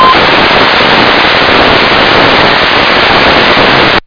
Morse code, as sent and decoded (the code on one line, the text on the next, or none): none